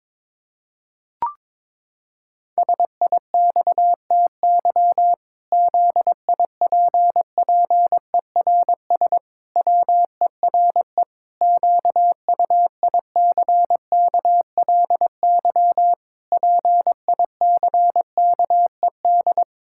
Morse code, heard as a keyed tone, about 22 wpm, 700 Hz. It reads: SIXTY ZIPPERS WERE QUICKLY PICKED
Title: Pangram Challenge v2 - 22wpm